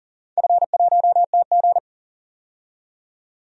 Morse code, heard as a keyed tone, about 37 wpm, 700 Hz. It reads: F1TG
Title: DXCC-100 Call Signs - 4 Characters - 40wpm